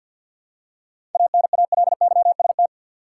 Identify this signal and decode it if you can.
Morse code, heard as a keyed tone, about 51 wpm, 700 Hz. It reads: ANALYST
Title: Seven Letter Words - 50wpm